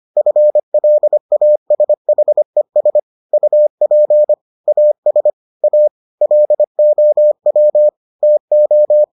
Morse code, heard as a keyed tone, about 25 wpm, 600 Hz. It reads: FLASHES UP AS A LOW TO